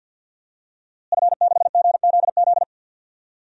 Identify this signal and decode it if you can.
Morse code, heard as a keyed tone, about 46 wpm, 700 Hz. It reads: F6GZZ
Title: DXCC-100 Call Signs - 5 Characters - Repeated 3x - 50wpm